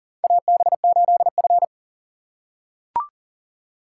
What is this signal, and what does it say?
Morse code, heard as a keyed tone, about 44 wpm, 700 Hz. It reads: AB8F
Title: USA Call Signs - 4 Characters - 6x Character Spacing - 40wpm